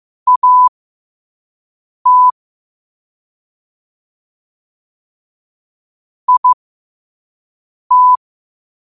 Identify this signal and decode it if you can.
Morse code, heard as a keyed tone, about 15 wpm, spaced as for 5 wpm, 1000 Hz. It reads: AT IT
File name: qotdfiles / 5wpm